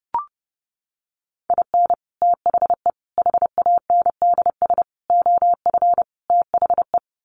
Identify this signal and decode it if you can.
Morse code, heard as a keyed tone, about 30 wpm, 700 Hz. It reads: IN THE HANDS OF THE
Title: Sets of 5 Words 30wpm